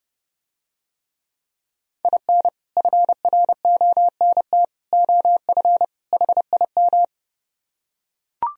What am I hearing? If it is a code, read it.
Morse code, heard as a keyed tone, about 30 wpm, 700 Hz. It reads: IN FRONT OF HIM